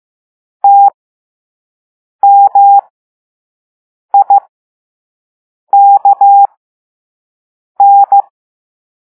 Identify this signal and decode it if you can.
Morse code, heard as a keyed tone, about 15 wpm, spaced as for 5 wpm, 800 Hz. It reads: TMIKN